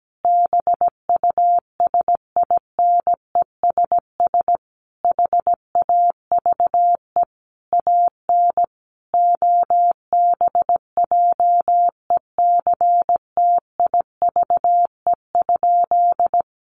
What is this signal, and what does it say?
Morse code, heard as a keyed tone, about 17 wpm, 700 Hz. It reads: BUSINESS HAVE AN OBJECTIVE?
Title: Sentences from Top 1200 Words 17wpm